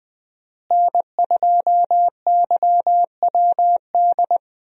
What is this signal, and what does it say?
Morse code, heard as a keyed tone, about 20 wpm, 700 Hz. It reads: N2YWD
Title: USA Call Signs - 5 Characters - Repeated 3x - 20wpm